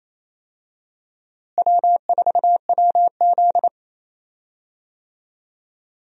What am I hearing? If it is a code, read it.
Morse code, heard as a keyed tone, about 28 wpm, 700 Hz. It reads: W4WZ
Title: USA Call Signs - 4 Characters - Repeated 3x - 28wpm